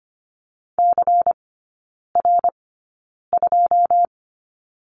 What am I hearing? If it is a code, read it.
Morse code, heard as a keyed tone, about 25 wpm, spaced as for 9 wpm, 700 Hz. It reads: CR2
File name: course / 25:10wpm / 093 Lesson 16e - Letter c - Three Characters 25wpm